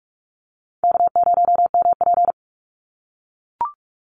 Morse code, heard as a keyed tone, about 44 wpm, 700 Hz. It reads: K0MP
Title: NA SOTA Chaser Callsigns 45wpm